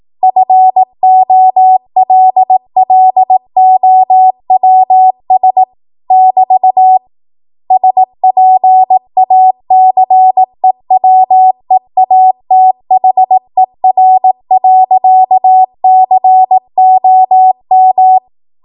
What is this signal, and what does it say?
Morse code, heard as a keyed tone, about 18 wpm, 750 Hz. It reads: FOLLOWS <BT> SPACEWEATHER.COM
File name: Morse / 18WPM